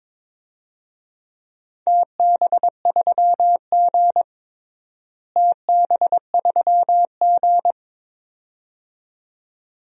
Morse code, heard as a keyed tone, about 22 wpm, 700 Hz. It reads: TB3G TB3G
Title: DXCC-100 Call Signs - 4 Characters - Repeated 2x - 22wpm